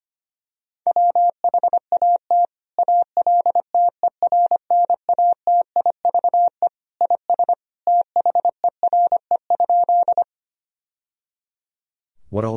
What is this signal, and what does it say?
Morse code, heard as a keyed tone, about 25 wpm, 700 Hz. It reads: WHAT ALTERNATIVE IS THERE?
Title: Sentences from Top 1000 Words - Review Part 4 - 25wpm